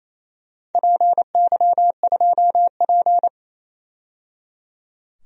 Morse code, heard as a keyed tone, about 28 wpm, 700 Hz. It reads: PY2P
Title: DXCC-100 Call Signs - 4 Characters - 28wpm